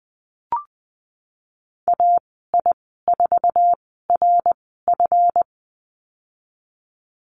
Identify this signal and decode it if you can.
Morse code, heard as a keyed tone, about 20 wpm, spaced as for 14 wpm, 700 Hz. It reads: AI4RF
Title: 154 Lesson 25g - Letter p - Call Signs 20wpm